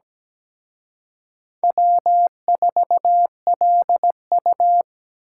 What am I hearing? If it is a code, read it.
Morse code, heard as a keyed tone, about 20 wpm, 700 Hz. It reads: W4LU
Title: Speed Racing - USA 1x2 Call Signs 17wpm